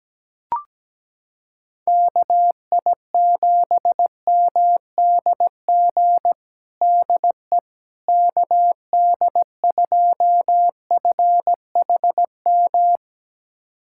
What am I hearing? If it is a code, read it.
Morse code, heard as a keyed tone, about 17 wpm, 700 Hz. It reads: KI7MDG DE KD2FHM
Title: USA Callsign DE Callsign 17wpm